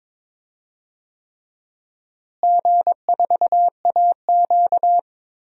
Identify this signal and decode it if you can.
Morse code, heard as a keyed tone, about 22 wpm, 700 Hz. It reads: G4AQ
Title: DXCC-100 Call Signs - 4 Characters - Repeated 3x - 22wpm